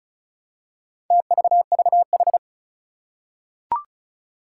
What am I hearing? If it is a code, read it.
Morse code, heard as a keyed tone, about 36 wpm, 700 Hz. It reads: TVVH
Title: Four Challenging Letter-Number Combinations 35wpm